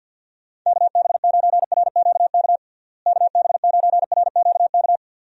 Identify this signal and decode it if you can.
Morse code, heard as a keyed tone, about 52 wpm, 700 Hz. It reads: KB9RQK KB9RQK
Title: USA Call Signs - 6 Characters - Repeated 2x - 50wpm